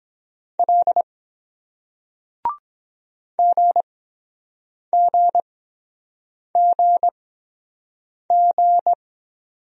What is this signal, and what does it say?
Morse code, heard as a keyed tone, about 22 wpm, spaced as for 7 wpm, 700 Hz. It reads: L GGGG